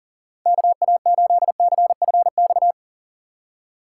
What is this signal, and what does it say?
Morse code, heard as a keyed tone, about 50 wpm, 700 Hz. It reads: KA8CFX
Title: Speed Racing - USA Call Signs 40wpm